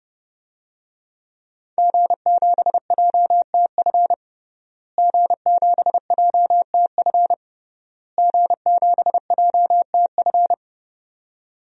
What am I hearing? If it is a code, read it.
Morse code, heard as a keyed tone, about 30 wpm, 700 Hz. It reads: G7JTF G7JTF G7JTF
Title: DXCC-100 Call Signs - 5 Characters - Repeated 3x - 30wpm